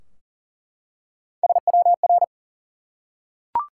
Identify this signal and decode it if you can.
Morse code, heard as a keyed tone, about 40 wpm, 700 Hz. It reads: SWR
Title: Top 180 QSO Elements 40wpm